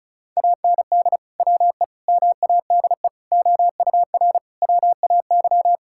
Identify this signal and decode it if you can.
Morse code, heard as a keyed tone, about 36 wpm, 700 Hz. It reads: AND WE MADE OUR WAY
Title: Sentences from Top 1600 Words 35wpm